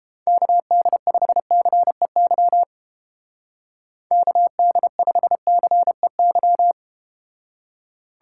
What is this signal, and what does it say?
Morse code, heard as a keyed tone, about 32 wpm, 700 Hz. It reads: KD5CEY KD5CEY